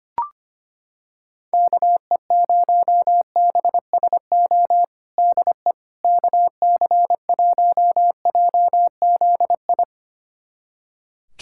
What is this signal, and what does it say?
Morse code, heard as a keyed tone, about 25 wpm, 700 Hz. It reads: KE0BSO DE KC1JZI
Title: USA Callsign DE Callsign - 6 Characters - 25wpm